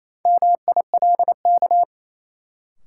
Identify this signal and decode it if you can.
Morse code, heard as a keyed tone, about 28 wpm, 700 Hz. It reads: MILK
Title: Sentences from Top 1500 Words - New Word Drill - 28wpm